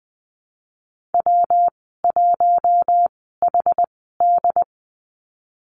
Morse code, heard as a keyed tone, about 20 wpm, spaced as for 15 wpm, 700 Hz. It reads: W1HD
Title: 122 Lesson 20g - Number 3 - Call Signs 20wpm